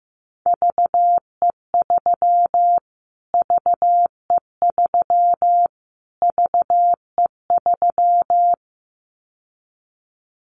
Morse code, heard as a keyed tone, about 15 wpm, 700 Hz. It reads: VE3 VE3 VE3
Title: Callsign Prefixes for Top 25 DXCC - Repeated 3x - 15wpm